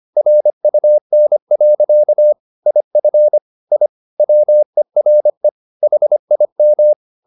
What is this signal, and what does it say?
Morse code, heard as a keyed tone, about 25 wpm, 600 Hz. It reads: RUN. IF I WERE HIM
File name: the old man and the sea part 1 25wpm